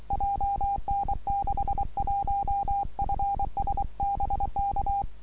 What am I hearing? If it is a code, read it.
Morse code, heard as a keyed tone, about 23 wpm, 800 Hz. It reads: JN61FS BK
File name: iw0ffk / 2012 09 02 OM6A IW0FFK